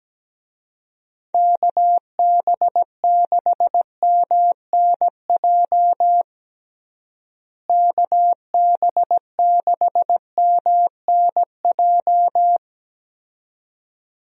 Morse code, heard as a keyed tone, about 17 wpm, 700 Hz. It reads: KB6MNJ KB6MNJ
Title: USA Call Signs - 6 Characters - Repeated 2x - 17wpm